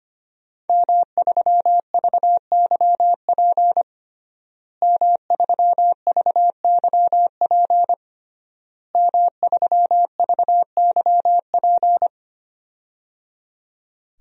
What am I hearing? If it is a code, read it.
Morse code, heard as a keyed tone, about 25 wpm, 700 Hz. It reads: M3VYP M3VYP M3VYP
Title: DXCC-100 Call Signs - 5 Characters - Repeated 3x - 25wpm